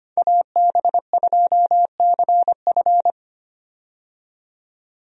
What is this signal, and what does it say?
Morse code, heard as a keyed tone, about 25 wpm, 700 Hz. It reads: AB2CF
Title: USA Call Signs - 5 Characters - 25wpm